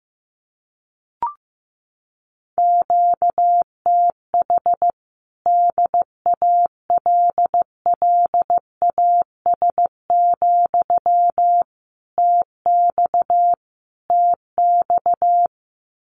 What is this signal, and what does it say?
Morse code, heard as a keyed tone, about 15 wpm, 700 Hz. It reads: QTH DALLAS, TX TX
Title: Common QSO Phrases Encore 15wpm